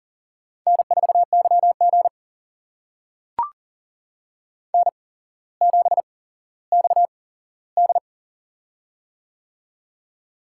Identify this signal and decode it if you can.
Morse code, heard as a keyed tone, about 40 wpm, spaced as for 17 wpm, 700 Hz. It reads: N4YG N7XD
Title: USA Call Signs - 4 Characters - 8x Character Spacing - 40wpm